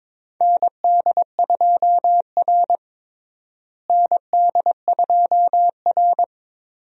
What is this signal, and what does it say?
Morse code, heard as a keyed tone, about 22 wpm, 700 Hz. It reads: ND2R ND2R
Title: USA Call Signs - 4 Characters - Repeated 3x - 22wpm